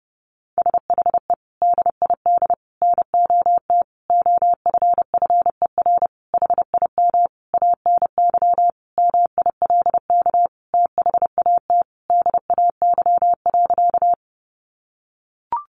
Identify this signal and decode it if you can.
Morse code, heard as a keyed tone, about 30 wpm, 700 Hz. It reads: SHE DID NOT OFFER HIM ANY MILK THAT DAY.
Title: Sentences from Top 1500 Words 30wpm